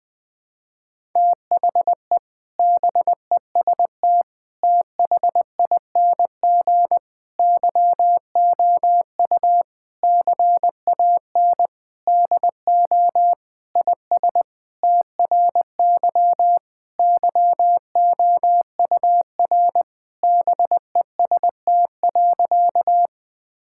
Morse code, heard as a keyed tone, about 20 wpm, 700 Hz. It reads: THE BEST THING YOU CAN DO IS TRY YOUR BEST.
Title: Sentences with 7-Word Phrases - 20wpm